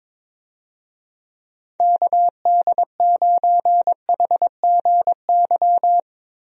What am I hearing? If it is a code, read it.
Morse code, heard as a keyed tone, about 22 wpm, 700 Hz. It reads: KD9HGY
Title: USA Call Signs - 6 Characters - Repeated 2x - 22wpm